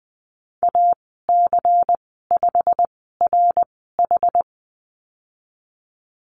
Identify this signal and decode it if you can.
Morse code, heard as a keyed tone, about 20 wpm, spaced as for 15 wpm, 700 Hz. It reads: AC5RH
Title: 154 Lesson 25g - Letter p - Call Signs 20wpm